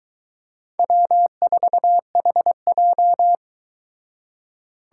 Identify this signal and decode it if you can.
Morse code, heard as a keyed tone, about 23 wpm, 700 Hz. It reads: W4HJ